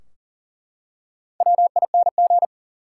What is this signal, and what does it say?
Morse code, heard as a keyed tone, about 40 wpm, 700 Hz. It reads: WING
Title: Four Letter Words - 40wpm